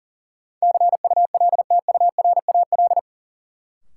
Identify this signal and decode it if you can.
Morse code, heard as a keyed tone, about 40 wpm, 700 Hz. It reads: CULTURAL
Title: Sentences from Top 1500 Words - New Word Drill - 40wpm